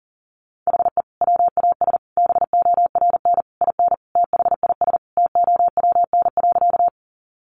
Morse code, heard as a keyed tone, about 40 wpm, 700 Hz. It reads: HE WAS BORN IN THIS TOWN.
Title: Sentences from Top 1000 Words 40wpm